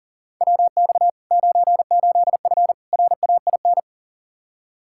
Morse code, heard as a keyed tone, about 40 wpm, 700 Hz. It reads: WX 98F RAIN
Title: Practice QSO - Exchange 2 - Info - 40wpm